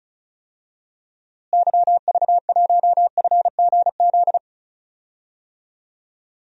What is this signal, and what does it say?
Morse code, heard as a keyed tone, about 36 wpm, 700 Hz. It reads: YV1FGZ
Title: DXCC-100 Call Signs - 6 Characters - Repeated 3x - 35wpm